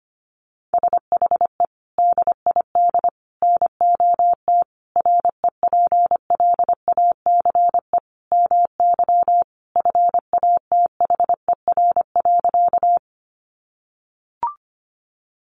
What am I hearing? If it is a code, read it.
Morse code, heard as a keyed tone, about 25 wpm, 700 Hz. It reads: SHE DID NOT REPLACE MY FATHER.
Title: Sentences from Top 1400 Words 25wpm